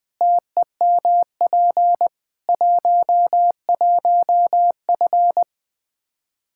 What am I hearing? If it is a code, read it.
Morse code, heard as a keyed tone, about 21 wpm, 700 Hz. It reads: TEMP 11F
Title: Practice QSO - Exchange 2 - Info - 20wpm